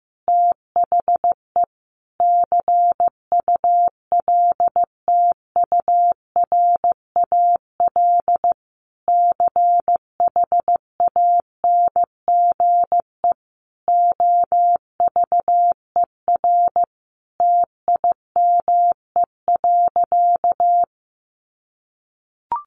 Morse code, heard as a keyed tone, about 15 wpm, 700 Hz. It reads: THE CULTURAL CHANGE OVER TIME.
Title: Sentences from Top 1500 Words 15wpm